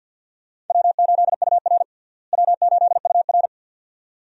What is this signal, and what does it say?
Morse code, heard as a keyed tone, about 52 wpm, 690 Hz. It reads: W8UR W8UR
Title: USA Call Signs - 4 Characters - Repeated 2x - 50wpm